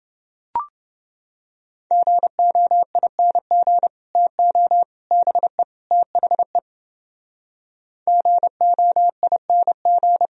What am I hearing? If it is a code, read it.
Morse code, heard as a keyed tone, about 30 wpm, 700 Hz. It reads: GOING TO BE THE GOING